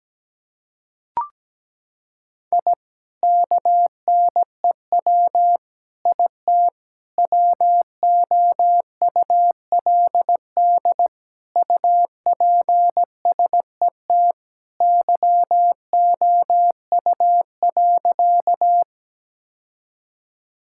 Morse code, heard as a keyed tone, about 17 wpm, 700 Hz. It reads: I KNEW IT WOULD UPSET YOU.
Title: Sentences from Top 1000 Words - Review Part 5 - 17wpm